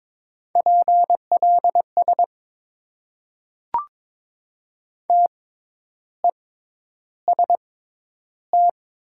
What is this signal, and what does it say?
Morse code, heard as a keyed tone, about 22 wpm, spaced as for 10 wpm, 700 Hz. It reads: PLS TEST
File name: Top 90 QSO Elements - 6x Character Spacing - 22wpm